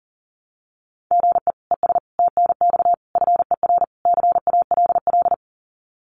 Morse code, heard as a keyed tone, about 40 wpm, 700 Hz. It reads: GE ES TNX FER CALL
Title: Common QSO Phrases Encore 40wpm